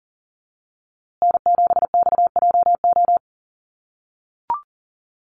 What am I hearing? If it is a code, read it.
Morse code, heard as a keyed tone, about 40 wpm, 700 Hz. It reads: N7XJO
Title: USA Call Signs Repeated 2x 40wpm